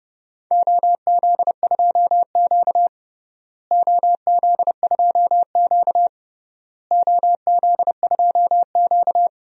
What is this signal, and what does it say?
Morse code, heard as a keyed tone, about 30 wpm, 700 Hz. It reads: OZ2Q OZ2Q OZ2Q
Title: DXCC-100 Call Signs - 4 Characters - Repeated 3x - 30wpm